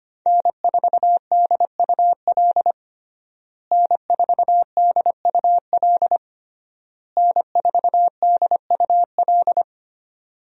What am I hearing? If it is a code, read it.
Morse code, heard as a keyed tone, about 25 wpm, 700 Hz. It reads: N4DUL N4DUL N4DUL
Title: USA Call Signs - 5 Characters - Repeated 3x - 25wpm